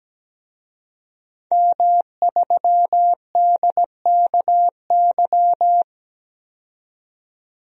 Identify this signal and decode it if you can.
Morse code, heard as a keyed tone, about 17 wpm, 700 Hz. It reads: M3DKY
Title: DXCC-100 Call Signs - 5 Characters - 17wpm